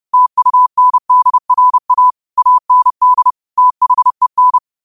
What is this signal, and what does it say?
Morse code, heard as a keyed tone, about 30 wpm, 1000 Hz. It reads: TUNDRA AND THEN
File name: qotdfiles / 30wpm